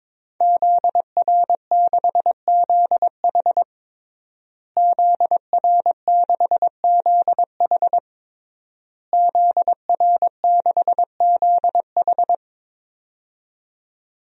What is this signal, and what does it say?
Morse code, heard as a keyed tone, about 22 wpm, 700 Hz. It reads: ZR6ZH ZR6ZH ZR6ZH
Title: DXCC-100 Call Signs - 5 Characters - Repeated 3x - 22wpm